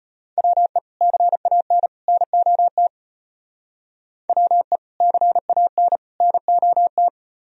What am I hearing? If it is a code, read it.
Morse code, heard as a keyed tone, about 33 wpm, 700 Hz. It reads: WE CAN NOT WE CAN NOT